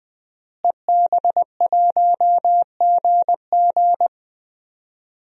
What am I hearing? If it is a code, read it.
Morse code, heard as a keyed tone, about 20 wpm, 700 Hz. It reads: EB1GG
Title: DXCC-100 Call Signs - 5 Characters - Repeated 2x - 20wpm